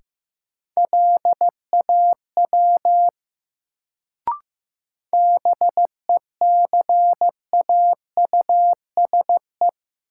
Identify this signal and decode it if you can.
Morse code, heard as a keyed tone, about 16 wpm, 700 Hz. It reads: LAW BECAUSE
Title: Sentences from Top 600 Words - New Word Drill - 15wpm